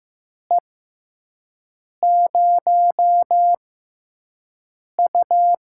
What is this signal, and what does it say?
Morse code, heard as a keyed tone, about 15 wpm, 700 Hz. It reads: E 0 U
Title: USA Call Signs - 6x Character Spacing - 15wpm